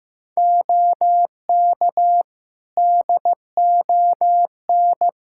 Morse code, heard as a keyed tone, about 15 wpm, 700 Hz. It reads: OK DON
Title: Practice QSO - Exchange 3 - More Info - 15wpm